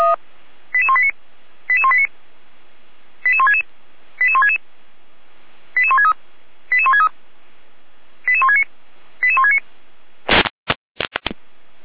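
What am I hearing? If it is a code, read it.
Five-tone selective call (ZVEI), heard as quick runs of five tones, tones of about 80 ms. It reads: 88189 88189 88166 88166 88163 88163 88169 88169
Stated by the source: Received with an SDR; ZVEI five-tone call 88191.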